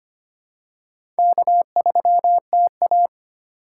Morse code, heard as a keyed tone, about 25 wpm, 700 Hz. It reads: K3TA
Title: USA Call Signs - 4 Characters - 4x Character Spacing - 25wpm